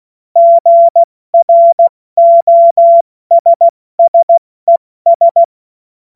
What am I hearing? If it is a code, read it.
Morse code, heard as a keyed tone, about 16 wpm, 680 Hz. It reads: GROSSES